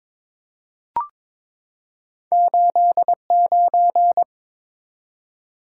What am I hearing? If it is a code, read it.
Morse code, heard as a keyed tone, about 22 wpm, 700 Hz. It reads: 89
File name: Two Letter-Number Combinations - 22wpm